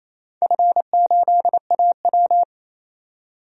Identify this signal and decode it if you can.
Morse code, heard as a keyed tone, about 28 wpm, 690 Hz. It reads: F8AW
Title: DXCC-100 Call Signs - 4 Characters - 28wpm